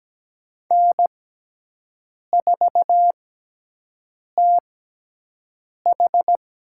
Morse code, heard as a keyed tone, about 17 wpm, spaced as for 6 wpm, 700 Hz. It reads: N4TH
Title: USA Call Signs - 6x Character Spacing - 17wpm